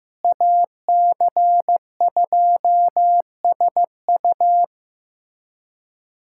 Morse code, heard as a keyed tone, about 15 wpm, 700 Hz. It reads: AC2SU
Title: USA Callsign DE Callsign - 5 Characters - 15wpm